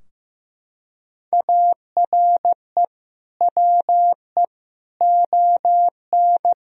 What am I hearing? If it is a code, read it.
Morse code, heard as a keyed tone, about 15 wpm, 700 Hz. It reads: ARE WE ON
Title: Sentences from Top 1000 Words - Review Part 3 - 15wpm